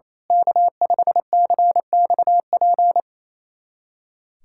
Morse code, heard as a keyed tone, about 27 wpm, 700 Hz. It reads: K5CXP
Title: USA Callsign DE Callsign - 5 Characters - 28wpm